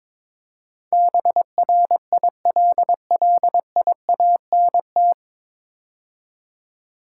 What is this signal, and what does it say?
Morse code, heard as a keyed tone, about 21 wpm, 700 Hz. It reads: BRILLIANT
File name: 9 Letter Words - First 3 Letters Given - 22wpm